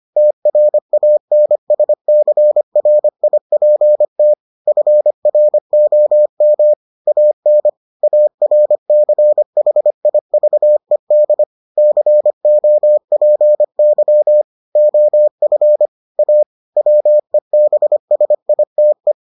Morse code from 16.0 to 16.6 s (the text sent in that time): A